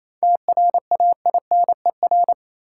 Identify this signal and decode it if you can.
Morse code, heard as a keyed tone, about 28 wpm, 700 Hz. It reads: TRAINER
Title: Seven Letter Words - 28wpm